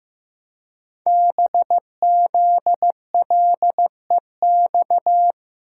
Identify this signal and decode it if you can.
Morse code, heard as a keyed tone, about 15 wpm, 700 Hz. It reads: BZLEX
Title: Code Groups - 5 Letters - 15wpm